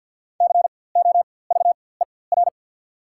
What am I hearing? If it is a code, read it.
Morse code, heard as a keyed tone, about 53 wpm, spaced as for 24 wpm, 690 Hz. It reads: COVER
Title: 4x Character Spacing - Top 300 Words - 50wpm